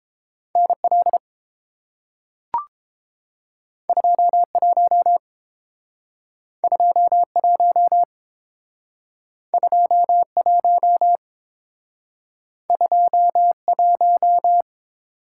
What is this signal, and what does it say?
Morse code, heard as a keyed tone, about 30 wpm, 700 Hz. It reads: NL 21 21 21 21